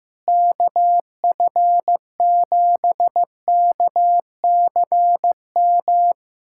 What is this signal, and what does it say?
Morse code, heard as a keyed tone, about 15 wpm, 700 Hz. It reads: KF7KCM
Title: USA Callsign DE Callsign - 6 Characters - 15wpm